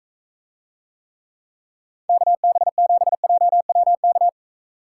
Morse code, heard as a keyed tone, about 43 wpm, 700 Hz. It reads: KB7JWK